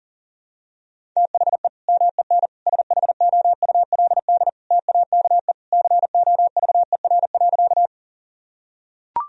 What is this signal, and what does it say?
Morse code, heard as a keyed tone, about 40 wpm, 700 Hz. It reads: THE MEN SHOULD TAKE COVER.
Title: Sentences from Top 1000 Words - Review Part 1 - 40wpm